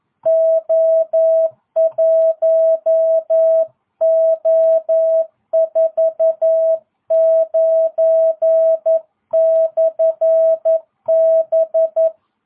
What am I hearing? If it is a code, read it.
Morse code, heard as a keyed tone, about 11 wpm, 650 Hz. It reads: O1O49/B